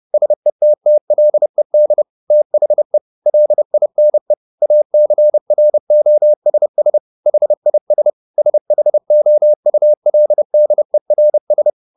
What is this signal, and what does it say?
Morse code, heard as a keyed tone, about 30 wpm, 600 Hz. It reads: SETTLED THE LINE ACROSS HIS SHOULDERS